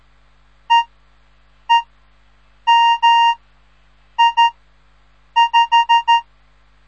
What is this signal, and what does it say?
Morse code, heard as a keyed tone, about 13 wpm, spaced as for 7 wpm, 930 Hz. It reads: EEMI5
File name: radiomorningstar / morse / les 3